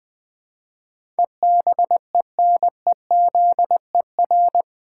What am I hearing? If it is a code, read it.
Morse code, heard as a keyed tone, about 20 wpm, 700 Hz. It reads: EBENEZER
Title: Pangram Challenge v2 - 20wpm